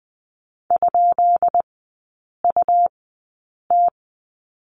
Morse code, heard as a keyed tone, about 20 wpm, spaced as for 8 wpm, 700 Hz. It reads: ?UT
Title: 133 Lesson 22e - Question Mark - Three Characters 20wpm